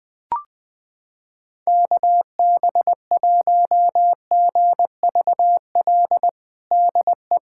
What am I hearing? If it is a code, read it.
Morse code, heard as a keyed tone, about 20 wpm, 700 Hz. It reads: KB1GVL DE
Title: USA Callsign DE Callsign - 6 Characters - 20wpm